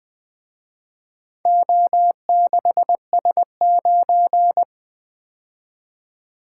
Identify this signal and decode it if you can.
Morse code, heard as a keyed tone, about 24 wpm, 700 Hz. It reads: O6S9
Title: Speed-Racing - Four Letter-Number Combinations - 20wpm